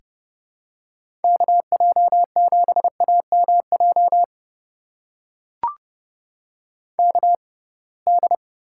Morse code, heard as a keyed tone, about 30 wpm, spaced as for 23 wpm, 700 Hz. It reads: KJ7AMJ KD
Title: USA Call Signs - 6 Characters - 6x Character Spacing - 30wpm